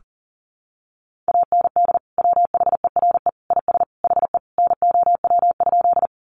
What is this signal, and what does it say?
Morse code, heard as a keyed tone, about 40 wpm, 700 Hz. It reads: AND WHERE IS HE NOW?
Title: Sentences from Top 200 Words 40wpm